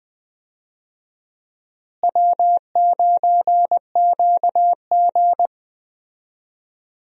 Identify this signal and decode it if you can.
Morse code, heard as a keyed tone, about 21 wpm, 700 Hz. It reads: W9QG
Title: USA Call Signs - 4 Characters - Repeated 2x - 20wpm